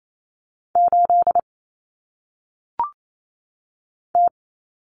Morse code, heard as a keyed tone, about 28 wpm, 700 Hz. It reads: 8 T
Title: Common Cut Numbers - 1-Digit 28wpm